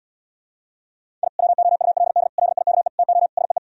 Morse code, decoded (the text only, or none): I WOULD LIKE TO SEE